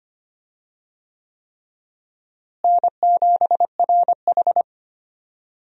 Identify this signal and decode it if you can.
Morse code, heard as a keyed tone, about 25 wpm, 700 Hz. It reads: N7RH
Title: USA Call Signs - 4 Characters - Repeated 3x - 25wpm